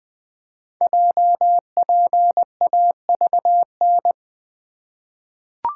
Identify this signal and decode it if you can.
Morse code, heard as a keyed tone, about 20 wpm, 700 Hz. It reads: JPAVN
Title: Code Groups - 5 Letters - 20wpm